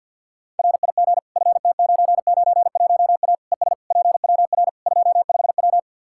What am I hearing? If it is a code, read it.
Morse code, heard as a keyed tone, about 49 wpm, 700 Hz. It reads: RIG FT991A ES PWR 25W
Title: Practice QSO - Exchange 2 - Info - 50wpm